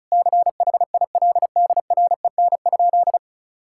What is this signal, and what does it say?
Morse code, heard as a keyed tone, about 34 wpm, 700 Hz. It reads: CHILDREN?
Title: Sentences from Top 1000 Words - Review Part 5 - 35wpm